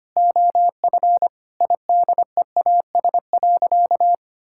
Morse code, heard as a keyed tone, about 25 wpm, 700 Hz. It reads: OF IDEAS.
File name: practice / 25wpm / Sentences with 4-Word Phrases - 25wpm